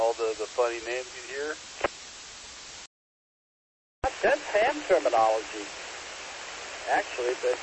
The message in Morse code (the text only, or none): none